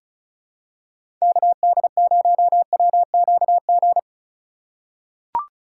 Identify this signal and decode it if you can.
Morse code, heard as a keyed tone, about 36 wpm, 700 Hz. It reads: KD0WQG
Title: USA Call Signs - 6 Characters - 2x Character Spacing - 35wpm